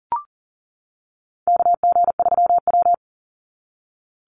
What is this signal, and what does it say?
Morse code, heard as a keyed tone, about 40 wpm, 700 Hz. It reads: KG3W
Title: NA SOTA Chaser Callsigns 40wpm